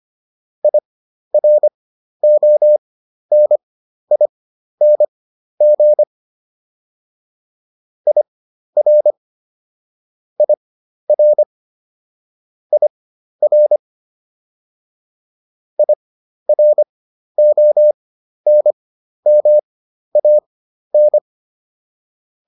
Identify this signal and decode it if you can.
Morse code, heard as a keyed tone, about 25 wpm, spaced as for 12 wpm, 600 Hz. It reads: IRONING IR IR IR IRONMAN